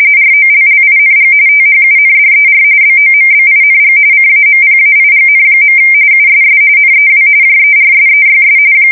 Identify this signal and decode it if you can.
RTTY, about 45.45 baud. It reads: ROLLING THUNDER BOMBING CAMPAIGN. USS NEW JERSEYS
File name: fsk170 / battleshipnj